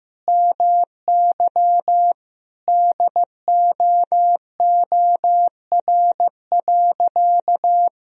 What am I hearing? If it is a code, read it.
Morse code, heard as a keyed tone, about 15 wpm, 700 Hz. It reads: MY DOOR.
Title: Sentences from Top 1000 Words - Review Part 1 - 15wpm